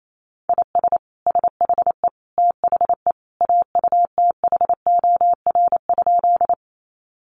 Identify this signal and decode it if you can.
Morse code, heard as a keyed tone, about 28 wpm, 700 Hz. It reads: IS SHE THE AUTHOR?
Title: Sentences from Top 1200 Words 28wpm